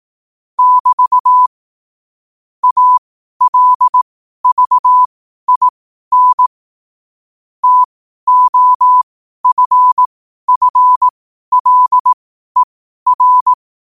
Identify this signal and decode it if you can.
Morse code, heard as a keyed tone, about 18 wpm, spaced as for 12 wpm, 1000 Hz. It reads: <BT> ALVIN TOFFLER
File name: qotdfiles / 12wpm